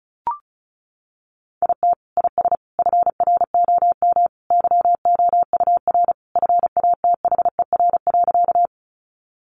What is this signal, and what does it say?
Morse code, heard as a keyed tone, about 34 wpm, 700 Hz. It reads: IT IS FROM YOUR FATHER.
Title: Sentences from Top 300 Words 35wpm